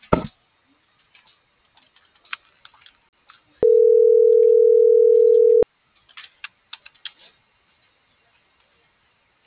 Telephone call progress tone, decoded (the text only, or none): ringback tone